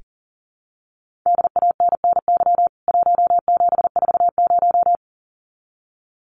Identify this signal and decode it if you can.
Morse code, heard as a keyed tone, about 40 wpm, 700 Hz. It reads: DANNY 1740
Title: CWT Exchange 40wpm